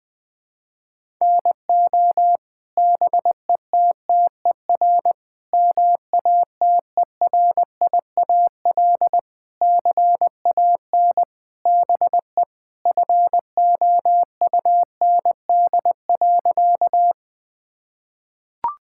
Morse code, heard as a keyed tone, about 20 wpm, 700 Hz. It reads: NO BETTER MATERIAL CAN BE FOUND.
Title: Sentences from Top 1000 Words - Review Part 5 - 20wpm